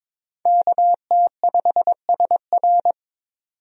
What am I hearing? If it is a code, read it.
Morse code, heard as a keyed tone, about 22 wpm, 700 Hz. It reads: KT5SR
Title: USA Call Signs - 5 Characters - 22wpm